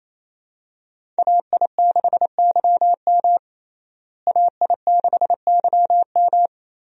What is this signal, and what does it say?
Morse code, heard as a keyed tone, about 28 wpm, 700 Hz. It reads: AI6YM AI6YM
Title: USA Call Signs - 5 Characters - Repeated 3x - 28wpm